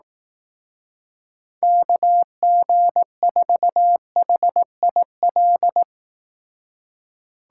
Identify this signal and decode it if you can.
Morse code, heard as a keyed tone, about 18 wpm, 700 Hz. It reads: KG4HIL
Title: Speed Racing - USA Call Signs 15wpm